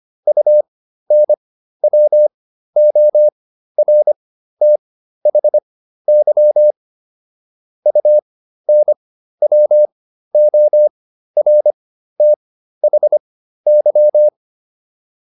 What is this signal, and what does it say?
Morse code, heard as a keyed tone, about 25 wpm, spaced as for 13 wpm, 600 Hz. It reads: UNWORTHY UNWORTHY